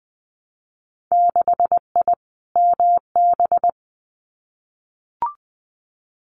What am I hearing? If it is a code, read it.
Morse code, heard as a keyed tone, about 20 wpm, 700 Hz. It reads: 6I MB
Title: Winter Field Day Exchanges Encore 20wpm